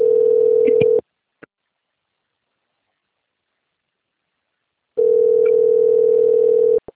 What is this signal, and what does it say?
Ringback tone (North America), 440 + 480 Hz.